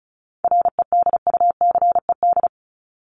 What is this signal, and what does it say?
Morse code, heard as a keyed tone, about 34 wpm, 700 Hz. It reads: REDUCED
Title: ED Suffix - Top 100 Words with Base Words 35wpm